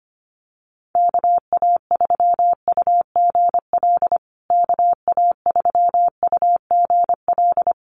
Morse code, heard as a keyed tone, about 25 wpm, 700 Hz. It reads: KA3UGL KA3UGL
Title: USA Call Signs Repeated 2x 25wpm